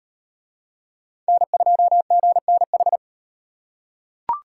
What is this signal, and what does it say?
Morse code, heard as a keyed tone, about 35 wpm, 700 Hz. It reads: N2GNH